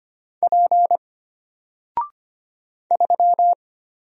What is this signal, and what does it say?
Morse code, heard as a keyed tone, about 25 wpm, 700 Hz. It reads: P 3
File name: Single-Letter-Number-25wpm